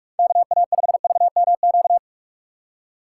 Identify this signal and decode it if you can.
Morse code, heard as a keyed tone, about 43 wpm, 700 Hz. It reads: KA5VMQ